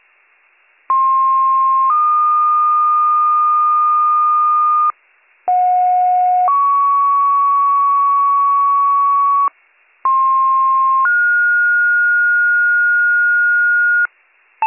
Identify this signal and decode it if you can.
Two-tone page, heard as a one-second tone, then a three-second tone, about 1060 then 1220 Hz, about 730 then 1120 Hz, about 1060 then 1430 Hz. It is heard from 0.9 to 14.1 s.